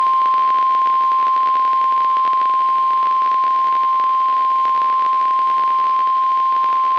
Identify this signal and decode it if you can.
Emergency Alert System (SAME) tones (the 1050 Hz attention tone). It is heard from 0.4 to 6.3 s.
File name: eas messages / EAS WXM95 Towanda